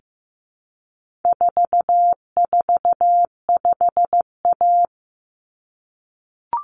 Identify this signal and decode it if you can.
Morse code, heard as a keyed tone, about 15 wpm, 700 Hz. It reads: 445A
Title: Common Cut Numbers - 4-Digit 15wpm